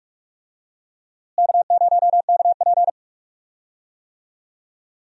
Morse code, heard as a keyed tone, about 45 wpm, 700 Hz. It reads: K0KP
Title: USA Call Signs - 4 Characters - Repeated 3x - 45wpm